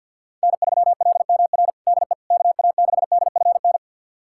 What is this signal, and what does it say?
Morse code, heard as a keyed tone, about 46 wpm, 700 Hz. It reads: N3LMR DE KA6DFN
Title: USA Callsign DE Callsign 50wpm